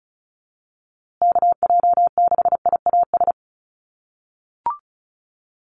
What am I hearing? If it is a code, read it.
Morse code, heard as a keyed tone, about 34 wpm, 700 Hz. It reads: KJ6IAS
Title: USA Call Signs Repeated 2x 35wpm